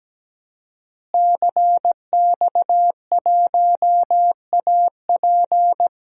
Morse code, heard as a keyed tone, about 17 wpm, 700 Hz. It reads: CX1AP
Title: DXCC-100 Call Signs - 5 Characters - Repeated 2x - 17wpm